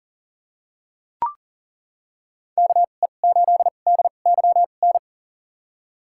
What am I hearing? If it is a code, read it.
Morse code, heard as a keyed tone, about 40 wpm, spaced as for 29 wpm, 700 Hz. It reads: KE8DYN
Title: USA Call Signs - 6 Characters - 2x Character Spacing - 40wpm